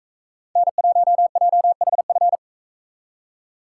Morse code, heard as a keyed tone, about 44 wpm, 700 Hz. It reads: N1JHF